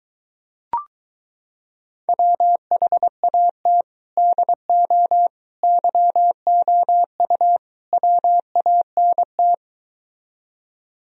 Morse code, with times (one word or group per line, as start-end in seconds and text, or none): 2.1-3.8 WHAT
4.2-5.3 DO
5.6-7.6 YOU
7.9-9.5 WANT